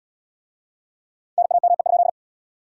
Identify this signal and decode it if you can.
Morse code, heard as a keyed tone, about 66 wpm, 700 Hz. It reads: NAME?